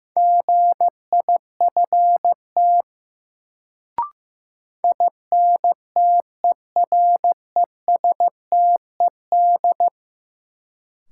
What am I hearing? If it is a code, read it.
Morse code, heard as a keyed tone, about 15 wpm, 700 Hz. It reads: GIFT INTERESTED
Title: Sentences from Top 900 Words - New Word Drill - 15wpm